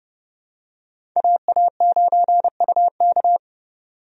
Morse code, heard as a keyed tone, about 32 wpm, 700 Hz. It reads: AA9UK
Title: USA Call Signs - 5 Characters - 30wpm